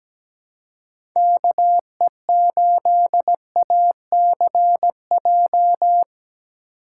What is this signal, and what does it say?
Morse code, heard as a keyed tone, about 17 wpm, 700 Hz. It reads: KE8ACJ
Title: USA Call Signs - 6 Characters - 4x Character Spacing - 17wpm